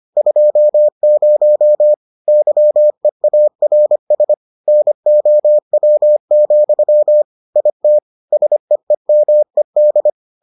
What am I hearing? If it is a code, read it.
Morse code, heard as a keyed tone, about 25 wpm, 600 Hz. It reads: 20 YEARS NOW, IT SEEMED